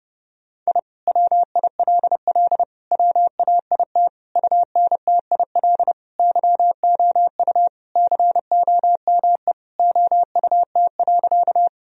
Morse code, heard as a keyed tone, about 32 wpm, 700 Hz. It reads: I WILL WAIT UNTIL YOU COME OUT.
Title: Sentences from Top 1000 Words - Review Part 1 - 30wpm